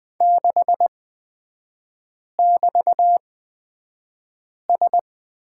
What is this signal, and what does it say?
Morse code, heard as a keyed tone, about 20 wpm, spaced as for 5 wpm, 700 Hz. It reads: 6<BT>S
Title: 236 Lesson 38e - Letter q - Three Characters 20wpm